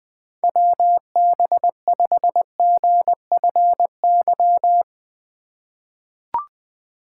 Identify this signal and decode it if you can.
Morse code, heard as a keyed tone, about 20 wpm, 700 Hz. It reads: WB5GFY
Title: USA Call Signs - 3x Character Spacing - 20wpm